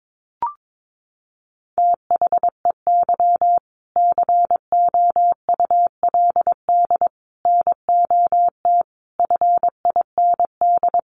Morse code, heard as a keyed tone, about 22 wpm, 700 Hz. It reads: THEY COULD NOT FIND
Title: Sentences from Top 1400 Words 22wpm